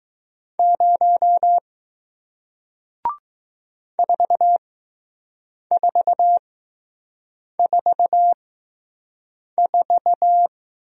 Morse code, heard as a keyed tone, about 20 wpm, spaced as for 6 wpm, 700 Hz. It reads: 0 4444